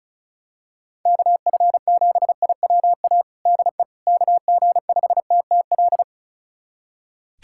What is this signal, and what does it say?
Morse code, heard as a keyed tone, about 36 wpm, 700 Hz. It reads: KF7IWA DE KG5TTL
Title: USA Callsign DE Callsign 35wpm